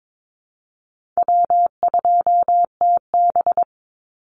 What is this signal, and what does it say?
Morse code, heard as a keyed tone, about 22 wpm, 700 Hz. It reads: W2TB
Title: Easy Call Signs Repeated 3x - 22wpm